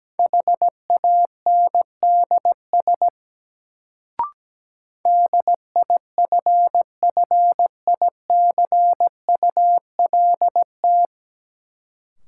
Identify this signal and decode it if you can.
Morse code, heard as a keyed tone, about 17 wpm, 700 Hz. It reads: HANDS DIFFICULT
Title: Sets of 4 Words - New Word Drill - 17wpm